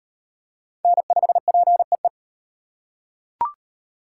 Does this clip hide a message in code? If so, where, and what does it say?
Yes: Morse code from 0.8 to 2.1 s, heard as a keyed tone, about 38 wpm, 700 Hz. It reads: N5PEE